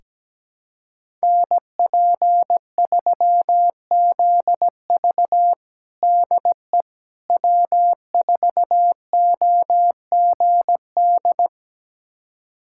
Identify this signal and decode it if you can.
Morse code, heard as a keyed tone, about 17 wpm, 700 Hz. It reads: NP3ZV DE W4OGD
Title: USA Callsign DE Callsign - 5 Characters - 17wpm